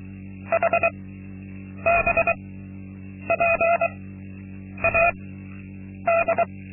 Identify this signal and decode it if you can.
Morse code, heard as a keyed tone, about 23 wpm, spaced as for 8 wpm, 680 Hz. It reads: HBPAD
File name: Linksmakalnis-2